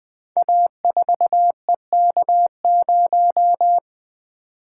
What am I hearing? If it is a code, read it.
Morse code, heard as a keyed tone, about 20 wpm, 700 Hz. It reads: A4EK0
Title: Code Groups - 5 Letter:Numbers - 20wpm